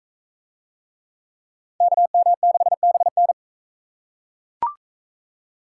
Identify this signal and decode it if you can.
Morse code, heard as a keyed tone, about 37 wpm, 700 Hz. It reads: KM6BN